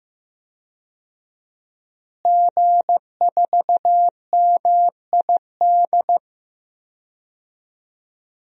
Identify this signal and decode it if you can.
Morse code, heard as a keyed tone, about 15 wpm, 700 Hz. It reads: G4MID
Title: DXCC-100 Call Signs - 5 Characters - Repeated 3x - 15wpm